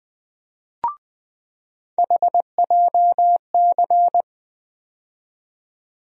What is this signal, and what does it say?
Morse code, heard as a keyed tone, about 20 wpm, 700 Hz. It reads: HJC
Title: USA Call Sign Postfixes - 3 Letters 20wpm